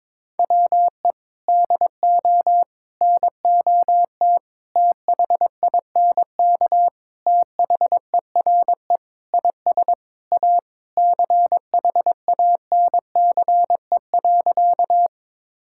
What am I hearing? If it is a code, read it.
Morse code, heard as a keyed tone, about 22 wpm, 700 Hz. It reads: WE DO NOT THINK THERE IS A CHANCE.
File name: practice / 22wpm / Sentences with 4-Word Phrases - 22wpm